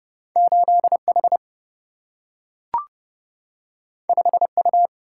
Morse code, heard as a keyed tone, about 30 wpm, 700 Hz. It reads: 8 H 5 U
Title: Single-Letter Single-Number 30wpm